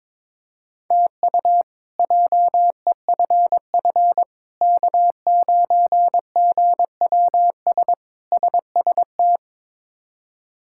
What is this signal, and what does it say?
Morse code, heard as a keyed tone, about 21 wpm, 700 Hz. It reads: TU JEFF K9GWS SST
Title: K1USN SST QSO Practice - 22wpm